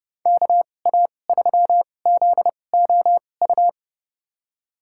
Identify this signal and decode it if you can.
Morse code, heard as a keyed tone, about 30 wpm, spaced as for 22 wpm, 700 Hz. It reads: KA3ZOU
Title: USA Call Signs - 6 Characters - 2x Character Spacing - 30wpm